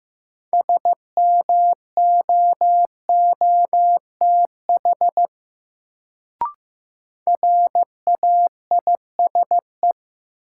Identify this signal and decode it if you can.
Morse code, heard as a keyed tone, about 15 wpm, 700 Hz. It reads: SMOOTH RAISE
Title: Sentences from Top 1400 Words - New Word Drill - 15wpm